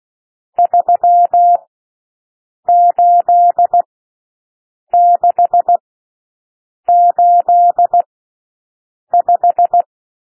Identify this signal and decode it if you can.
Morse code, heard as a keyed tone, about 16 wpm, spaced as for 6 wpm, 700 Hz. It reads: 38685
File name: SOUND NUMBERS 16 wpm